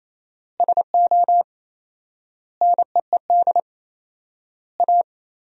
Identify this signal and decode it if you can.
Morse code, heard as a keyed tone, about 28 wpm, 700 Hz. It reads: SO NEED A
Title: 4x Word-Spacing - Sentences from Top 200 Words - 28wpm